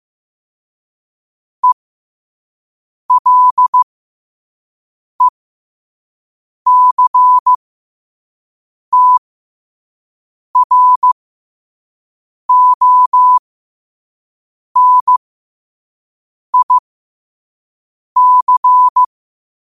Morse code, heard as a keyed tone, about 15 wpm, spaced as for 5 wpm, 1000 Hz. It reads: ELECTRONIC
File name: qotdfiles / 5wpm